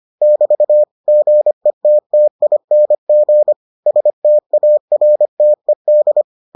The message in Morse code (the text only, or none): <BT> GETTING STARTED